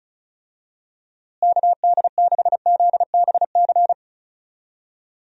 Morse code, heard as a keyed tone, about 36 wpm, 700 Hz. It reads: KD6ZBC
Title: USA Call Signs - 6 Characters - Repeated 3x - 35wpm